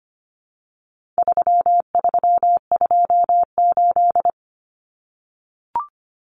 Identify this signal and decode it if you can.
Morse code, heard as a keyed tone, about 25 wpm, 700 Hz. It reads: 3328
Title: Common Cut Numbers - 4-Digit 25wpm